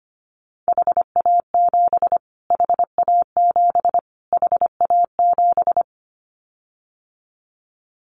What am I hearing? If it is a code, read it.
Morse code, heard as a keyed tone, about 25 wpm, 700 Hz. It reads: HA7 HA7 HA7
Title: Callsign Prefixes for Top 25 DXCC - Repeated 3x - 25wpm